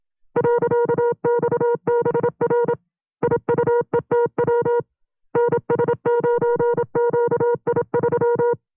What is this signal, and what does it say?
Morse code, heard as a keyed tone, about 27 wpm, 480 Hz. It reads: .XBR IUETW NS9QI3
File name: FreeFiles / CW-practice / 25WPM-side2